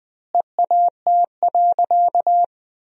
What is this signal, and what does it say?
Morse code, heard as a keyed tone, about 20 wpm, 700 Hz. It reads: EAT.
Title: 3x Word-Spacing - Sentences from Top 300 Words - 20wpm